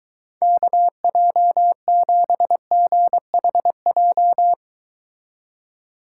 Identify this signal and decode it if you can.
Morse code, heard as a keyed tone, about 23 wpm, 700 Hz. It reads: KJ7GHJ